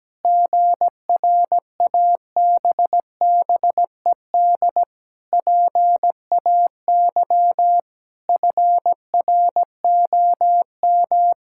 Morse code, heard as a keyed tone, about 17 wpm, 700 Hz. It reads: GRABBED PAY FROM
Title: Pangram Challenge v2 - 17wpm